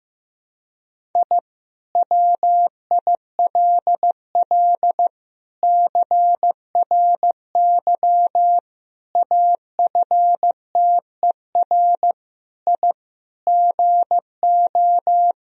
Morse code, heard as a keyed tone, about 15 wpm, 700 Hz. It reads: I WILL CRY AFTER I GO
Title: Sentences from Top 1000 Words - Review Part 2 - 15wpm